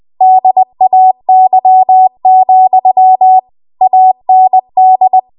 Morse code, heard as a keyed tone, about 20 wpm, 750 Hz. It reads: DAY, AND
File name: Morse / 20WPM